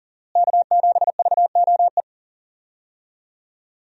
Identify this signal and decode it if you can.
Morse code, heard as a keyed tone, about 40 wpm, 700 Hz. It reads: K7VOE
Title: USA Call Signs - 5 Characters - Repeated 2x - 40wpm